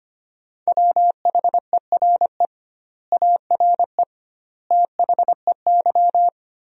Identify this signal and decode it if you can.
Morse code, heard as a keyed tone, about 25 wpm, 700 Hz. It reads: WHERE ARE THEY
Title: 2x Word-Spacing - Sentences from Top 200 Words - 25wpm